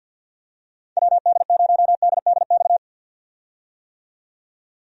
Morse code, heard as a keyed tone, about 47 wpm, 700 Hz. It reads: WD0DDX
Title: USA Call Signs - 6 Characters - 50wpm